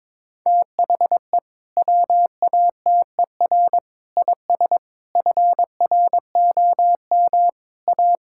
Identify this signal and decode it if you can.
Morse code, heard as a keyed tone, about 22 wpm, 700 Hz. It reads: THE WATER IS FROM A
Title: Sentences from Top 1000 Words - Review Part 5 - 22wpm